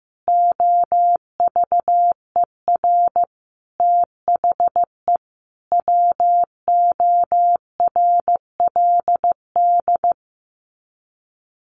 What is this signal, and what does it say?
Morse code, heard as a keyed tone, about 15 wpm, 700 Hz. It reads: OVER THE WORLD
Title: Sets of 5 Words 15wpm